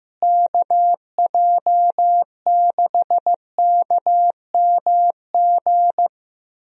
Morse code, heard as a keyed tone, about 15 wpm, 700 Hz. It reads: KJ6KMG
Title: USA Call Signs - 6 Characters - Repeated 2x - 15wpm